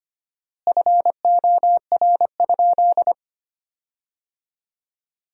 Morse code, heard as a keyed tone, about 25 wpm, 700 Hz. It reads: FOR?
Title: 4x Word-Spacing - Sentences from Top 100 Words - 25wpm